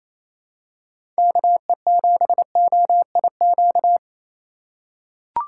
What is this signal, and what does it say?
Morse code, heard as a keyed tone, about 28 wpm, 700 Hz. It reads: KE7OIQ
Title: USA Call Signs - 6 Characters - 3x Character Spacing - 28wpm